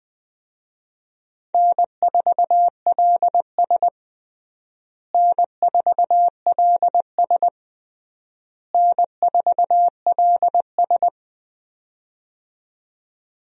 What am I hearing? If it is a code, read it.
Morse code, heard as a keyed tone, about 21 wpm, 700 Hz. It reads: N4LS N4LS N4LS
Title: USA Call Signs - 4 Characters - Repeated 3x - 20wpm